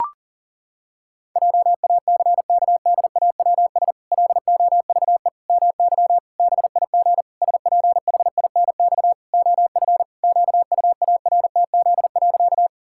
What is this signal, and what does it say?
Morse code, heard as a keyed tone, about 40 wpm, 700 Hz. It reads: JACKDAWS LOVE MY BIG SPHINX OF QUARTZ.
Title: Pangram Challenge v2 - 40wpm